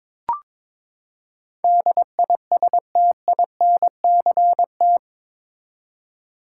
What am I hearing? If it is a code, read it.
Morse code, heard as a keyed tone, about 21 wpm, 700 Hz. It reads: DISTINCT
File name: practice / 22wpm / Eight Letter Words - 22wpm